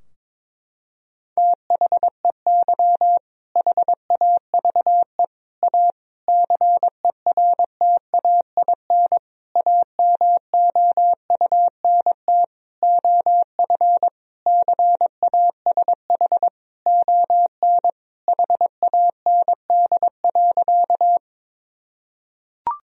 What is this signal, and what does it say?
Morse code, heard as a keyed tone, about 22 wpm, 700 Hz. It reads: THEY HAVE A CERTAIN AMOUNT OF CASH ON HAND.
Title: Sentences from Top 1000 Words - Review Part 4 - 22wpm